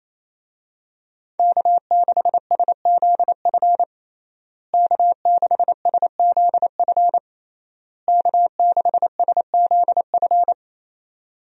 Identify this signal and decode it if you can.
Morse code, heard as a keyed tone, about 28 wpm, 700 Hz. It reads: K6SZF K6SZF K6SZF
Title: USA Call Signs - 5 Characters - Repeated 3x - 28wpm